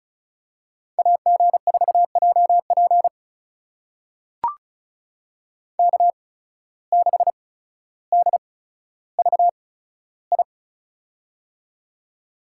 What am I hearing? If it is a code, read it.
Morse code, heard as a keyed tone, about 34 wpm, spaced as for 15 wpm, 690 Hz. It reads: AG4JP K6DVI
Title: USA Call Signs - 5 Characters - 8x Character Spacing - 35wpm